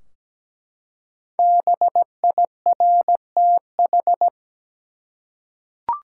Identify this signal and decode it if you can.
Morse code, heard as a keyed tone, about 17 wpm, 700 Hz. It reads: BIRTH
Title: Five Letter Words - 17wpm